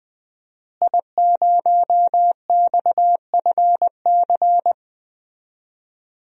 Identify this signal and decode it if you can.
Morse code, heard as a keyed tone, about 20 wpm, 700 Hz. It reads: I0XFC
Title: DXCC-100 Call Signs - 5 Characters - Repeated 3x - 20wpm